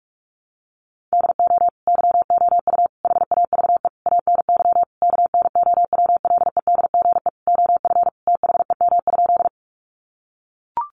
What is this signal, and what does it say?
Morse code, heard as a keyed tone, about 44 wpm, 700 Hz. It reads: DO YOU HAVE ANY KNOWLEDGE OF THEM?
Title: Sentences from Top 600 Words 45wpm